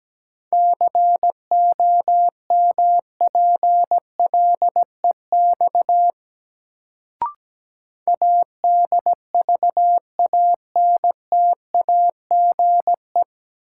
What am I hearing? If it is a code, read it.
Morse code, heard as a keyed tone, about 17 wpm, 700 Hz. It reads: COMPLEX ADVANTAGE
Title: Sentences from Top 900 Words - New Word Drill - 17wpm